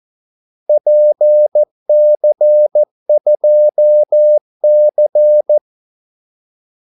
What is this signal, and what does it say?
Morse code, heard as a keyed tone, about 14 wpm, 600 Hz. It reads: PC2C
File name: Callsigns-14wpm